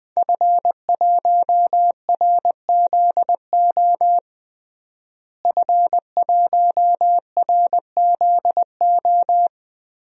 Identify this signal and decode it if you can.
Morse code, heard as a keyed tone, about 21 wpm, 700 Hz. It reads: F1RZO F1RZO
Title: DXCC-100 Call Signs - 5 Characters - Repeated 2x - 20wpm